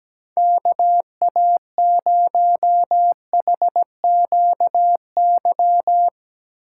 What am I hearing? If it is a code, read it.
Morse code, heard as a keyed tone, about 19 wpm, 700 Hz. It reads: KA0HQY
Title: Speed Racing - USA 2x3 Call Signs 17wpm